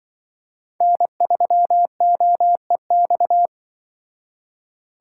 Morse code, heard as a keyed tone, about 25 wpm, 700 Hz. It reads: N3OEX
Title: Speed Racing - USA Call Signs 20wpm